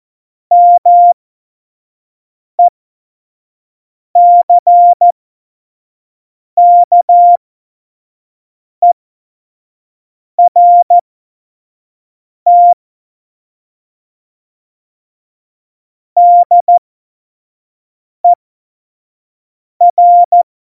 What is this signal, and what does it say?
Morse code, heard as a keyed tone, about 14 wpm, spaced as for 5 wpm, 700 Hz. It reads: MECKERT DER
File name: Morsekurs / Witze / Witz 14. Januar 2020, 25 F70